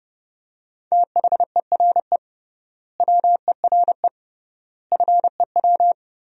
Morse code, heard as a keyed tone, about 30 wpm, 700 Hz. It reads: THERE WERE FEW
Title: 3x Word-Spacing - Sentences from Top 300 Words - 30wpm